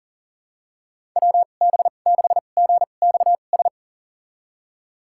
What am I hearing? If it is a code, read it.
Morse code, heard as a keyed tone, about 40 wpm, spaced as for 29 wpm, 690 Hz. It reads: WB6GXS
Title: USA Call Signs - 6 Characters - 2x Character Spacing - 40wpm